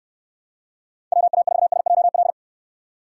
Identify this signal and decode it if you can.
Morse code, heard as a keyed tone, about 69 wpm, 700 Hz. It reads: WA4SJL